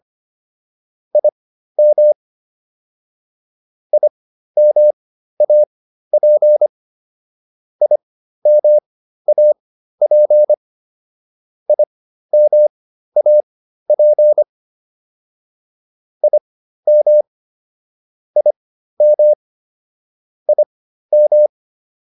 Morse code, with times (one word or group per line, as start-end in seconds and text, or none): 1.2-2.1 IM
3.9-6.7 IMAP
7.8-10.5 IMAP
11.7-14.4 IMAP
16.2-17.2 IM
18.4-19.3 IM
20.5-21.5 IM